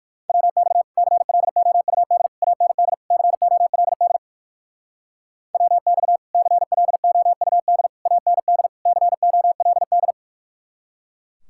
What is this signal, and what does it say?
Morse code, heard as a keyed tone, about 59 wpm, 700 Hz. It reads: WX CLOUD AND COLD WX CLOUD AND COLD